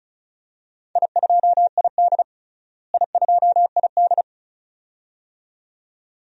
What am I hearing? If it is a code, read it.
Morse code, heard as a keyed tone, about 36 wpm, 700 Hz. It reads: I2ID I2ID
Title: DXCC-100 Call Signs - 4 Characters - Repeated 2x - 35wpm